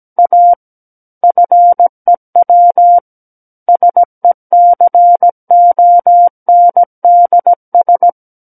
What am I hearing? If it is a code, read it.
Morse code, heard as a keyed tone, about 17 wpm, 700 Hz. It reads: A FEW SECONDS
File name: Book One / 08-Chapter 8 - 17 WPM